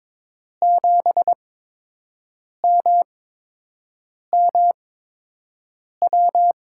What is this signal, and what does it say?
Morse code, heard as a keyed tone, about 22 wpm, spaced as for 6 wpm, 700 Hz. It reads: 7MMW
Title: USA Call Signs - 6 Characters - 8x Character Spacing - 22wpm